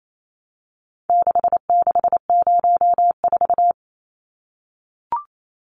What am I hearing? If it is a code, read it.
Morse code, heard as a keyed tone, about 28 wpm, 700 Hz. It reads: 6604
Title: Numbers - 4-Digit 28wpm